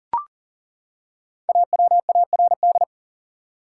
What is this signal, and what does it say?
Morse code, heard as a keyed tone, about 37 wpm, 700 Hz. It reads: AWARD
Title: Five Letter Words - 40wpm